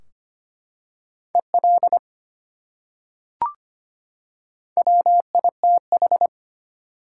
Morse code, heard as a keyed tone, about 25 wpm, 700 Hz. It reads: EL WITH
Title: Top 180 QSO Elements 25wpm